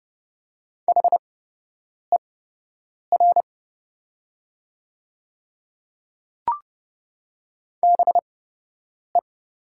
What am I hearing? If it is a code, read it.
Morse code, heard as a keyed tone, about 30 wpm, spaced as for 8 wpm, 700 Hz. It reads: HER BE
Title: 8x Character Spacing - Top 100 Words - 30wpm